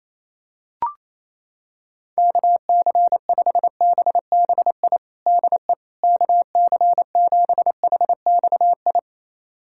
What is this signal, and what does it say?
Morse code, heard as a keyed tone, about 28 wpm, 700 Hz. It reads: KC5BBI DE KC7HXI
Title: USA Callsign DE Callsign - 6 Characters - 28wpm